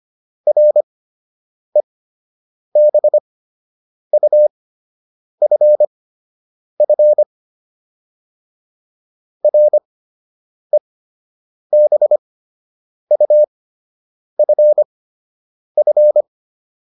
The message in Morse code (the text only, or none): REBUFF REBUFF